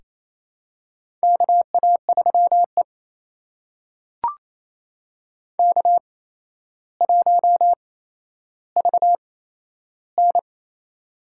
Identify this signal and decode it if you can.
Morse code, heard as a keyed tone, about 28 wpm, spaced as for 12 wpm, 700 Hz. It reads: KA3E K1VN
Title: USA Call Signs - 4 Characters - 8x Character Spacing - 28wpm